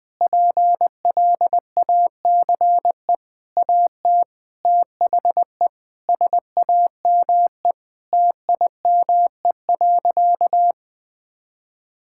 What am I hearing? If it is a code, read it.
Morse code, heard as a keyed tone, about 20 wpm, 700 Hz. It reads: PLACE AT THE SAME TIME.
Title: Sentences with 8-Word Phrases - 20wpm